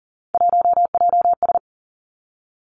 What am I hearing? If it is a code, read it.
Morse code, heard as a keyed tone, about 40 wpm, 700 Hz. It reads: 1JS
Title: Three Challenging Letter-Number Combinations 40wpm